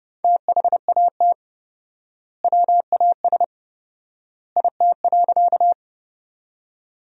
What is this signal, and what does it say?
Morse code, heard as a keyed tone, about 32 wpm, 700 Hz. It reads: THAT WAS IT.
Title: 4x Word-Spacing - Sentences from Top 100 Words - 30wpm